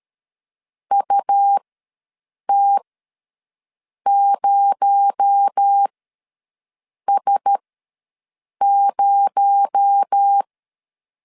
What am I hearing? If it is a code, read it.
Morse code, heard as a keyed tone, about 13 wpm, spaced as for 6 wpm, 800 Hz. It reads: UT0S0
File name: LEZIONI / 6 PKX0 / 6 PKX0 ws 12wpm 3448